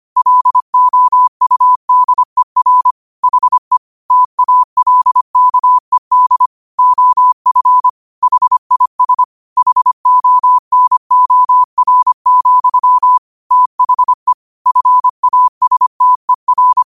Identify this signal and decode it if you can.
Morse code, heard as a keyed tone, about 25 wpm, 1000 Hz. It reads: LOUDER HE TALKED OF HIS HONOR, THE FASTER
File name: qotdfiles / 25wpm